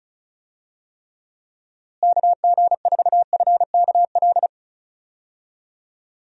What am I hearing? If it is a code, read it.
Morse code, heard as a keyed tone, about 34 wpm, 700 Hz. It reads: KG4FKL
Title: USA Call Signs - 6 Characters - Repeated 2x - 35wpm